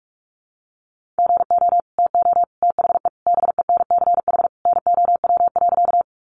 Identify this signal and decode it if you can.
Morse code, heard as a keyed tone, about 44 wpm, 700 Hz. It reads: GO TO THE BENCH NOW.
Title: Sentences from Top 1400 Words 45wpm